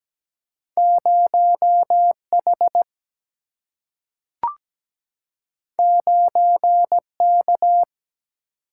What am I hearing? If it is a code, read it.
Morse code, heard as a keyed tone, about 17 wpm, 700 Hz. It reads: 0H 9K
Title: Single-Letter Single-Number 17wpm